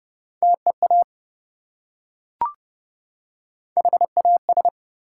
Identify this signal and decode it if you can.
Morse code, heard as a keyed tone, about 30 wpm, 700 Hz. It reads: TEA HAS
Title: Three Letter Words - 30wpm